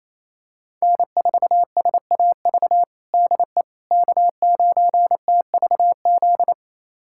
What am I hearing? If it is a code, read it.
Morse code, heard as a keyed tone, about 28 wpm, 700 Hz. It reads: N4SAV DE K9TVZ
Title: USA Callsign DE Callsign - 5 Characters - 28wpm